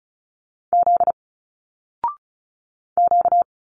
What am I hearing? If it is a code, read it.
Morse code, heard as a keyed tone, about 36 wpm, 700 Hz. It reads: Z Q
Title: Single Letters 35wpm